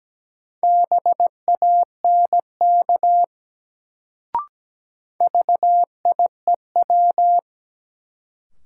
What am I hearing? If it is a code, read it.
Morse code, heard as a keyed tone, about 17 wpm, 700 Hz. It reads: BANK VIEW
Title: Sentences from Top 900 Words - New Word Drill - 17wpm